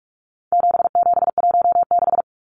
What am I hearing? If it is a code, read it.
Morse code, heard as a keyed tone, about 44 wpm, 700 Hz. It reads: 7716
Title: Numbers - 4-Digit 45wpm